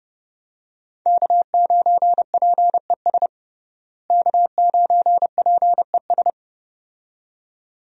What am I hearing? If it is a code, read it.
Morse code, heard as a keyed tone, about 32 wpm, 700 Hz. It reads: K9PES K9PES
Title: USA Call Signs - 5 Characters - Repeated 2x - 30wpm